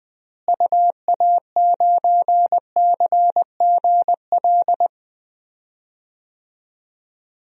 Morse code, heard as a keyed tone, about 21 wpm, 700 Hz. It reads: UA9CGL
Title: DXCC-100 Call Signs - 6 Characters - Repeated 3x - 20wpm